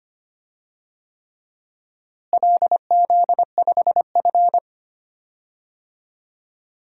Morse code, heard as a keyed tone, about 25 wpm, 700 Hz. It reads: LZ5F
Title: DXCC-100 Call Signs - 4 Characters - Repeated 3x - 25wpm